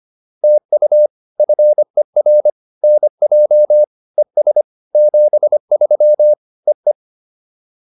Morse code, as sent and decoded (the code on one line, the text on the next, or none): - ..- / ..-. .-. / -. .--- / . ... / --... ...-- / . .
TU FER NJ ES 73 EE